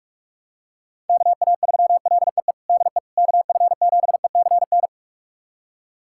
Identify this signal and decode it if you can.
Morse code, heard as a keyed tone, about 44 wpm, 690 Hz. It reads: KA3LEE DE KF7ECN
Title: USA Callsign DE Callsign - 6 Characters - 45wpm